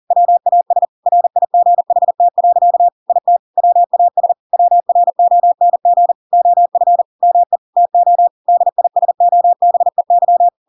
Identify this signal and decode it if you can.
Morse code, heard as a keyed tone, about 40 wpm, 700 Hz. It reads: WAS RIGHT. IT WAS WRONG OF ME TO DISOBEY
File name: Pinocchio / Pinocchio ch5 40 40 ik1hge